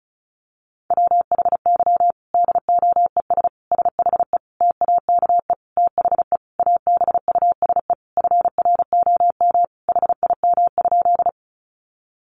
Morse code, heard as a keyed tone, about 34 wpm, 700 Hz. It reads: WHY DOES SHE TAKE THE ABUSE FROM HIM?
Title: Sentences from Top 1300 Words 35wpm